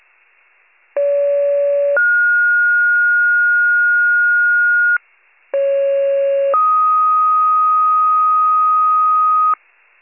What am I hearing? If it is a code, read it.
Two-tone page, heard as a one-second tone, then a three-second tone, about 570 then 1430 Hz, about 550 then 1250 Hz, with noise. It is heard from 1.0 to 9.5 s.
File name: pager / 29 FIRE / wcalert